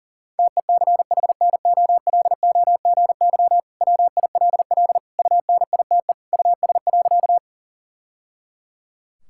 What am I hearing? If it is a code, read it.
Morse code, heard as a keyed tone, about 37 wpm, 700 Hz. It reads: TECHNOLOGY WILL UNITE US.
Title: Sentences from Top 1000 Words - Review Part 5 - 40wpm